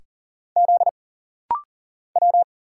Morse code, heard as a keyed tone, about 39 wpm, 700 Hz. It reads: Z W
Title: Single Letters - Rapid-Fire 40wpm